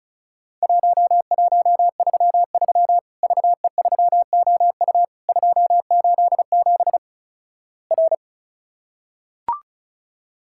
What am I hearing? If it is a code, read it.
Morse code, heard as a keyed tone, about 36 wpm, 700 Hz. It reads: 1133 VE3OU 287 R
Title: Worked All Europe DX Contest - QTC Exchange - 35wpm